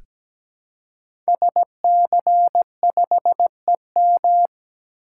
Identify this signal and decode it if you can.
Morse code, heard as a keyed tone, about 17 wpm, 700 Hz. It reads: SC5EM
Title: Code Groups - 5 Letter:Numbers - 17wpm